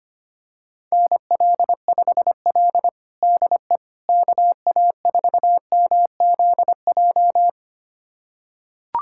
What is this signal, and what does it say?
Morse code, heard as a keyed tone, about 25 wpm, 700 Hz. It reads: NL5L DE KA4MZJ
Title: USA Callsign DE Callsign 25wpm